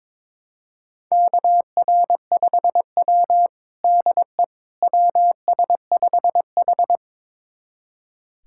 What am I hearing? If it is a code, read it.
Morse code, heard as a keyed tone, about 22 wpm, 700 Hz. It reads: KR5W DE WS5H
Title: USA Callsign DE Callsign - 4 Characters - 22wpm